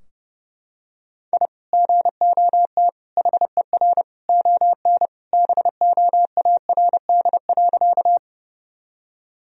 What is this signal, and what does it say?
Morse code, heard as a keyed tone, about 30 wpm, 700 Hz. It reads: I GOT HER ON BOARD.
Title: Sentences from Top 1000 Words - Review Part 4 - 30wpm